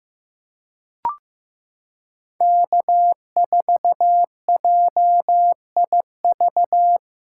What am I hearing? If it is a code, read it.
Morse code, heard as a keyed tone, about 15 wpm, 700 Hz. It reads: K4JIV
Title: USA Callsign DE Callsign - 5 Characters - 15wpm